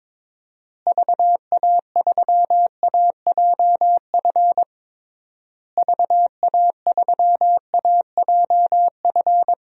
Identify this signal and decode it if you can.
Morse code, heard as a keyed tone, about 22 wpm, 700 Hz. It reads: VA3AJF VA3AJF
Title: DXCC-100 Call Signs - 6 Characters - Repeated 3x - 22wpm